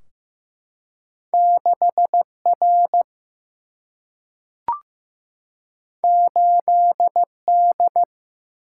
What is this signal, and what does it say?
Morse code, heard as a keyed tone, about 15 wpm, 700 Hz. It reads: 6R 8D
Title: Single-Letter Single-Number 15wpm